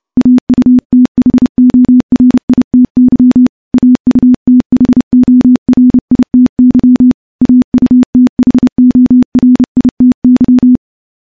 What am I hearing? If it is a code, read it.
Morse code, heard as a keyed tone, about 30 wpm, 250 Hz. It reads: AUTHORITY AUTHORITY AUTHORITY
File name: morse / morse-training-30WPM